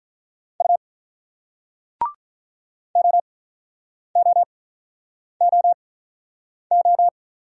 Morse code, heard as a keyed tone, about 48 wpm, spaced as for 9 wpm, 690 Hz. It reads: U OOOO